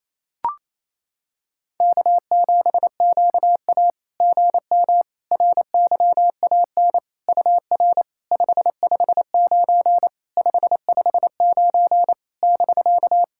Morse code, heard as a keyed tone, about 29 wpm, 690 Hz. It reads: K7QA GM RYAN UR 559 559 <BK>
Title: SOTA QSO Practice 28wpm